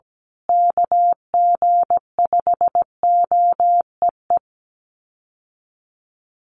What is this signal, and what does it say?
Morse code, heard as a keyed tone, about 17 wpm, 700 Hz. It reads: KG5OEE
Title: USA Call Signs Repeated 2x 17wpm